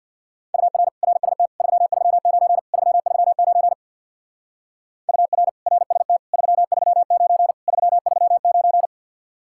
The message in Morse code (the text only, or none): UR RST 339 339 UR RST 339 339